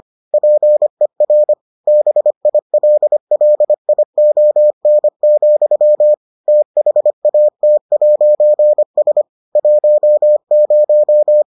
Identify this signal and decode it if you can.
Morse code, heard as a keyed tone, about 25 wpm, 600 Hz. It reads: PER BILLION, THAT'S 10